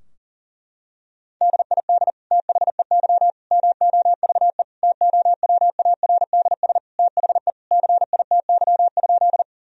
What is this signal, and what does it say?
Morse code, heard as a keyed tone, about 37 wpm, 700 Hz. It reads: DID THEY MOVE TOWARDS THE CITY?
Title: Sentences from Top 1000 Words - Review Part 2 - 40wpm